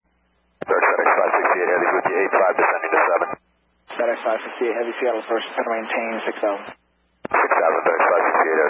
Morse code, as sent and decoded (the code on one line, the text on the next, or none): none